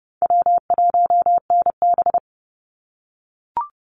Morse code, heard as a keyed tone, about 30 wpm, 700 Hz. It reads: W1NB
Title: USA Call Signs 30wpm